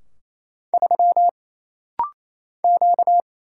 Morse code, heard as a keyed tone, about 28 wpm, 700 Hz. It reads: 3 Q
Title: Single Letter-Number - Rapid-Fire 28wpm